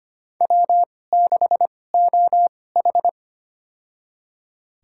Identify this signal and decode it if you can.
Morse code, heard as a keyed tone, about 25 wpm, spaced as for 18 wpm, 700 Hz. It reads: W6OH
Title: USA Call Signs - 4 Characters - 2x Character Spacing - 25wpm